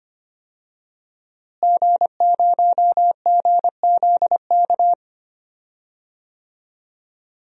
Morse code, heard as a keyed tone, about 25 wpm, 700 Hz. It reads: G0GZK
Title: DXCC-100 Call Signs - 5 Characters - Repeated 3x - 25wpm